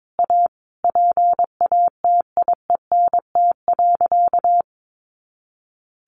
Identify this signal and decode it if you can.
Morse code, heard as a keyed tone, about 22 wpm, 700 Hz. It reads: A PATIENT.
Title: Additional Easy Phrases 22wpm